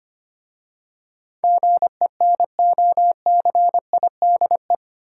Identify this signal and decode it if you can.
Morse code, heard as a keyed tone, about 25 wpm, 700 Hz. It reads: GENOCIDE
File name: 8 Letter Words - First 3 Letters Given - 25wpm